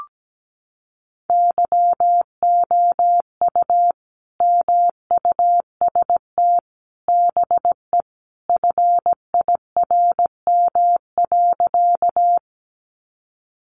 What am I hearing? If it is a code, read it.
Morse code, heard as a keyed tone, about 17 wpm, 700 Hz. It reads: YOU MUST BE FIRM.
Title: Sentences from Top 1400 Words 17wpm